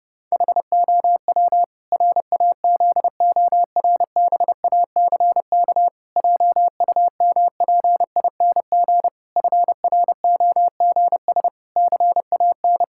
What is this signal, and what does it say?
Morse code, heard as a keyed tone, about 30 wpm, 700 Hz. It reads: HOW RAZORBACK JUMPING FROGS CAN
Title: Pangram Challenge v2 - 30wpm